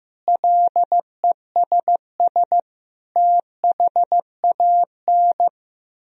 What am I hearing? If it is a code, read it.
Morse code, heard as a keyed tone, about 15 wpm, 700 Hz. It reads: LESS THAN
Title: Sentences from Top 1000 Words - Review Part 2 - 15wpm